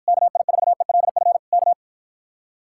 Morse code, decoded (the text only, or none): KI4ELF K